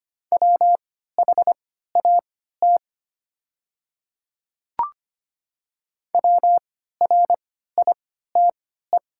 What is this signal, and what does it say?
Morse code, heard as a keyed tone, about 25 wpm, spaced as for 14 wpm, 700 Hz. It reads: WHAT WRITE
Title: 3x Character Spacing - Top 100 Words - 25wpm